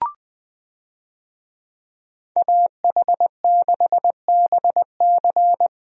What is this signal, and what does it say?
Morse code, heard as a keyed tone, about 21 wpm, 700 Hz. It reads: AH6BC
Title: USA Call Signs - 5 Characters - 20wpm